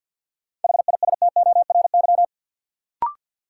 Morse code, heard as a keyed tone, about 53 wpm, 700 Hz. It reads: HISTORY
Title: Sets of 3 Words - New Word Drill - 50wpm